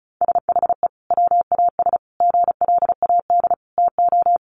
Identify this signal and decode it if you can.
Morse code, heard as a keyed tone, about 34 wpm, 700 Hz. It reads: SHE WAS GLAD TO
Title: Sentences from Top 800 Words 35wpm